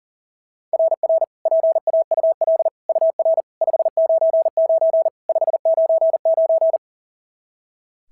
Morse code, heard as a keyed tone, about 41 wpm, 690 Hz. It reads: RR PAUL UR 599 599
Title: SOTA QSO Practice 40wpm